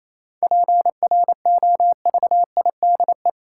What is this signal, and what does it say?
Morse code, heard as a keyed tone, about 28 wpm, 700 Hz. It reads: PROVIDE
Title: Sentences from Top 1100 Words - New Word Drill Alt - 28wpm